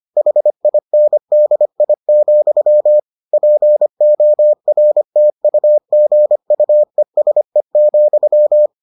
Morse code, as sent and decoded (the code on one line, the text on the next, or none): .... .. -. -.. .. --..-- / .--. --- .-. - ..- --. ..- . ... --..--
HINDI, PORTUGUESE,